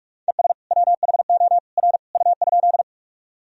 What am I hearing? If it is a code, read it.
Morse code, heard as a keyed tone, about 43 wpm, 700 Hz. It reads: ES WHO R U?
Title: Sentences from Top 100 Words and Abbreviations 45wpm